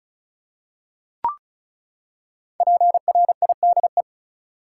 Morse code, heard as a keyed tone, about 34 wpm, 700 Hz. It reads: PRIDE
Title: Five Letter Words - 35wpm